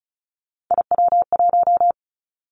Morse code, heard as a keyed tone, about 36 wpm, 700 Hz. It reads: IW1
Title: Callsign Prefixes for Top 50 DXCC - 35wpm